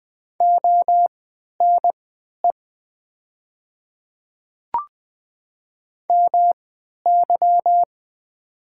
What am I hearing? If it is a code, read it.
Morse code, heard as a keyed tone, about 20 wpm, spaced as for 11 wpm, 700 Hz. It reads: ONE MY
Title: 3x Character Spacing - Top 100 Words - 20wpm